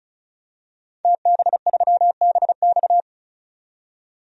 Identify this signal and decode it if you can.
Morse code, heard as a keyed tone, about 34 wpm, 700 Hz. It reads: TB3BX
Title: DXCC-100 Call Signs - 5 Characters - 35wpm